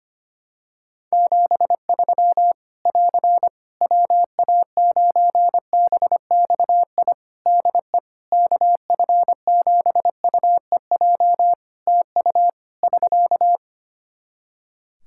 Morse code, heard as a keyed tone, about 25 wpm, 700 Hz. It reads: 73 <AR> WA9BXI DE KF7UEJ TU <SK>
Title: K1USN SST QSO Practice 25wpm